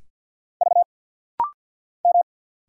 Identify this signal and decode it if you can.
Morse code, heard as a keyed tone, about 49 wpm, 700 Hz. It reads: V M
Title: Single Letters - Rapid-Fire 50wpm